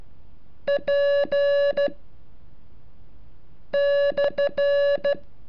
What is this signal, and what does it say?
Morse code, heard as a keyed tone, about 13 wpm, 570 Hz. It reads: P /
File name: curso / ejercicio 35